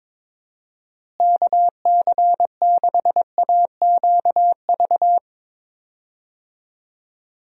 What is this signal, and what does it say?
Morse code, heard as a keyed tone, about 22 wpm, 700 Hz. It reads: KC6AQV
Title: USA Call Signs - 6 Characters - Repeated 2x - 22wpm